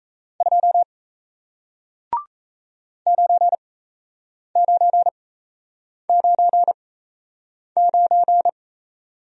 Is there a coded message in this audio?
Yes: Morse code from 0.4 to 8.5 s, heard as a keyed tone, about 37 wpm, spaced as for 8 wpm, 700 Hz. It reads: 2 9999